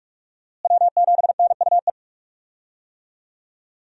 Morse code, heard as a keyed tone, about 46 wpm, 700 Hz. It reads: W7NUE
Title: USA Call Signs - 5 Characters - Repeated 2x - 45wpm